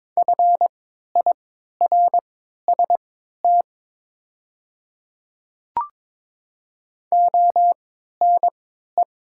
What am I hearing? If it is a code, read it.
Morse code, heard as a keyed tone, about 21 wpm, spaced as for 12 wpm, 700 Hz. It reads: FIRST ONE
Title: 3x Character Spacing - Top 100 Words - 22wpm